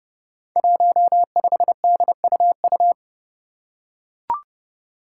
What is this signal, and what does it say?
Morse code, heard as a keyed tone, about 30 wpm, 700 Hz. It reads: 15DUU
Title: Five Challenging Letter-Number Combinations 30wpm